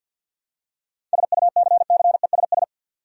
Morse code, heard as a keyed tone, about 50 wpm, 700 Hz. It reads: SUCCESS
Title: Seven Letter Words - 50wpm